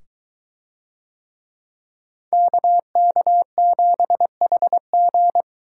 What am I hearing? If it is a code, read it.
Morse code, heard as a keyed tone, about 21 wpm, 700 Hz. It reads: KK7HG